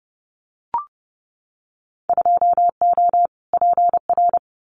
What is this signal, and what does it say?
Morse code, heard as a keyed tone, about 32 wpm, 700 Hz. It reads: 2O PR
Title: Winter Field Day Exchanges Encore 30wpm